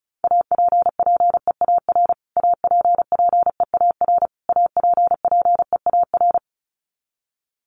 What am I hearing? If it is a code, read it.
Morse code, heard as a keyed tone, about 34 wpm, 700 Hz. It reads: APPEAR APPEAR APPEAR
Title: Top 500 Words Repeated 35wpm